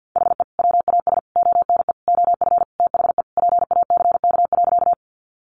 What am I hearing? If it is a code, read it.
Morse code, heard as a keyed tone, about 50 wpm, 700 Hz. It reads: HE WAS ONE OF THE PACK.
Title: Sentences from Top 1300 Words 50wpm